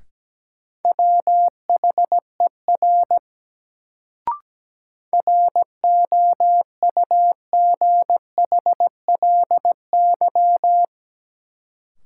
Morse code, heard as a keyed tone, about 17 wpm, 700 Hz. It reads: WHER ROUGHLY
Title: Sentences from Top 1500 Words - New Word Drill - 17wpm